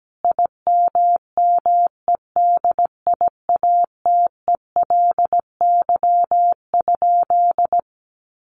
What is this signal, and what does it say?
Morse code, heard as a keyed tone, about 17 wpm, 700 Hz. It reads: IMMEDIATELY?
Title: Sentences from Top 1300 Words 17wpm